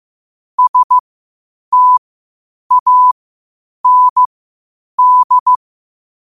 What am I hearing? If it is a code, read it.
Morse code, heard as a keyed tone, about 15 wpm, spaced as for 8 wpm, 1000 Hz. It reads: STAND
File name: qotdfiles / 8wpm